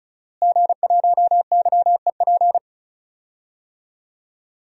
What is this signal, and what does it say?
Morse code, heard as a keyed tone, about 34 wpm, 700 Hz. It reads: G1YEP
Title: DXCC-100 Call Signs - 5 Characters - Repeated 2x - 35wpm